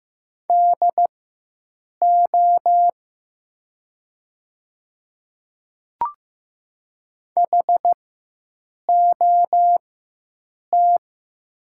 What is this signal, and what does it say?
Morse code, heard as a keyed tone, about 15 wpm, spaced as for 7 wpm, 700 Hz. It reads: DO HOT
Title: 4x Character Spacing - Top 100 Words - 15wpm